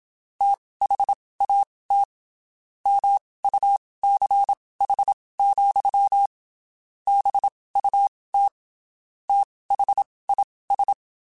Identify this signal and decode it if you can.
Morse code, heard as a keyed tone, about 27 wpm, spaced as for 19 wpm, 800 Hz. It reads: THAT MUCH, BUT THIS
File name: (spaced) / QSOs Spaced-mp3s / WD0FBT @25 sp